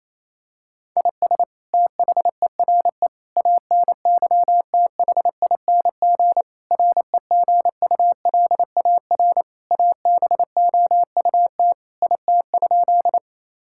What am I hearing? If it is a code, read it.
Morse code, heard as a keyed tone, about 28 wpm, 700 Hz. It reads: IS THERE ANYTHING REGULAR ABOUT IT?
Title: Sentences from Top 1000 Words - Review Part 6 - 28wpm